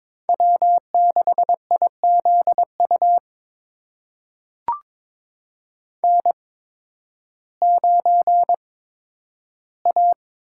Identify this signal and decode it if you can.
Morse code, heard as a keyed tone, about 22 wpm, spaced as for 12 wpm, 700 Hz. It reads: W6IZU N9A
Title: USA Call Signs - 5 Characters - 8x Character Spacing - 22wpm